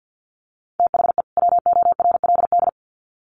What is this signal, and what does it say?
Morse code, heard as a keyed tone, about 50 wpm, 700 Hz. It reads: THE WORLD
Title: Sets of 2 Words 50wpm